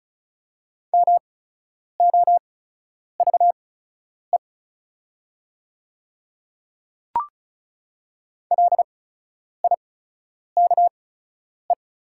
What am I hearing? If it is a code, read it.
Morse code, heard as a keyed tone, about 36 wpm, spaced as for 10 wpm, 690 Hz. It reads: MOVE LIKE
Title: 8x Character Spacing - Top 300 Words - 35wpm